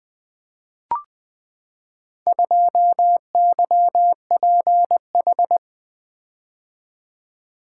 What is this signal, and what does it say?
Morse code, heard as a keyed tone, about 20 wpm, 700 Hz. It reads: 2YPH
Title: Four Letter-Number Combinations - 20wpm